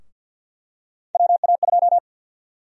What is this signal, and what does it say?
Morse code, heard as a keyed tone, about 52 wpm, 700 Hz. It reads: WA2
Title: USA Call Sign Prefixes - 50wpm